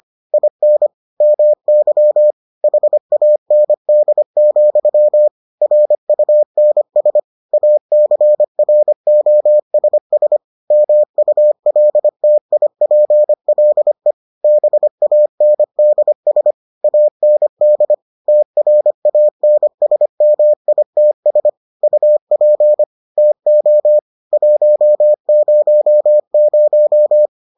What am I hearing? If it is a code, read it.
Morse code, heard as a keyed tone, about 25 wpm, 600 Hz. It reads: IN MY HAND, RUNS ACROSS MULTIPLE BANDS AND TRANSMITS UP TO 100